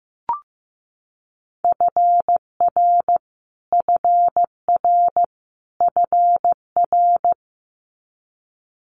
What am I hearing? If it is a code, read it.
Morse code, heard as a keyed tone, about 15 wpm, 700 Hz. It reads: FR FR FR
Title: Multi-letter Phonemes Repeated 3x 15wpm